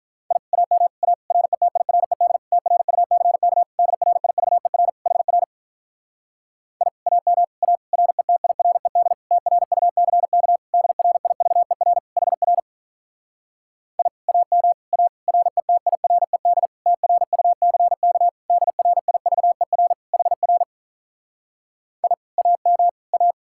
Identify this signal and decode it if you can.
Morse code, heard as a keyed tone, about 47 wpm, 700 Hz. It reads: I AM A RETIRED TRUCK DRIVER HR I AM A RETIRED TRUCK DRIVER HR I AM A RETIRED TRUCK DRIVER HR I AM A